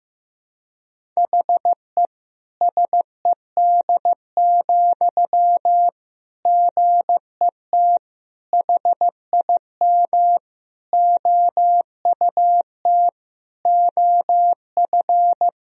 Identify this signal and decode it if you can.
Morse code, heard as a keyed tone, about 15 wpm, 700 Hz. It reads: HE SED, GET HIM OUT OF
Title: Sentences from Top 400 Words and Abbreviations 15wpm